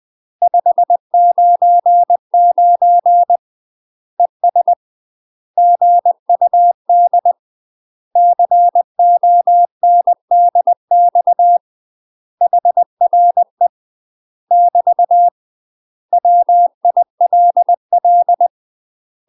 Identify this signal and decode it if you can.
Morse code, heard as a keyed tone, about 20 wpm, 700 Hz. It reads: 599 ES GUD CONDX HRE <BT> WILL
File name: UK MASTER QSOs TRAINING / 20WPM / UK MASTER QSOs 0027